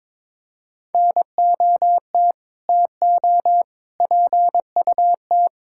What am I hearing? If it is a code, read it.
Morse code, heard as a keyed tone, about 22 wpm, 700 Hz. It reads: NOT TO PUT
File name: practice / 22wpm / Sentences with 8-Word Phrases - 22wpm